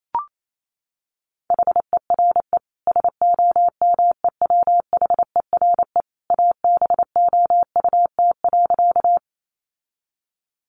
Morse code, heard as a keyed tone, about 28 wpm, 700 Hz. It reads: HERE SOMEWHERE ABOUT.
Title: Sentences from Top 1400 Words 28wpm